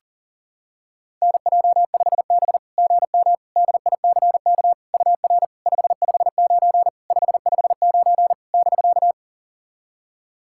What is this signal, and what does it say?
Morse code, heard as a keyed tone, about 40 wpm, 690 Hz. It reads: NJ5B GM DICK UR 559 559 <BK>
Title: SOTA QSO Practice 40wpm